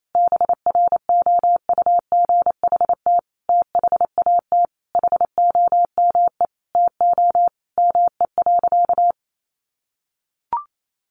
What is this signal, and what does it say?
Morse code, heard as a keyed tone, about 28 wpm, 700 Hz. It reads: BROUGHT THAT HOME TO ME.
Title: Sentences from Top 1100 Words 28wpm